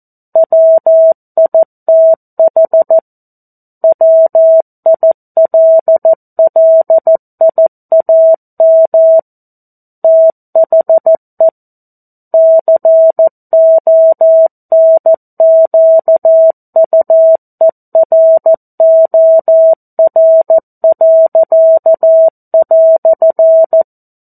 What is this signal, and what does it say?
Morse code, heard as a keyed tone, about 14 wpm, 650 Hz. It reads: WITH WILLIAM THE CONQUEROR."
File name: 02-Chapter-02 14 WPM